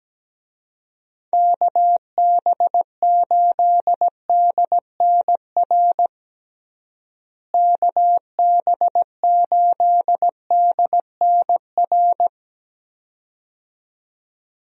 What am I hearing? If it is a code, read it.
Morse code, heard as a keyed tone, about 17 wpm, 700 Hz. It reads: KB8DNR KB8DNR
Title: USA Call Signs - 6 Characters - Repeated 2x - 17wpm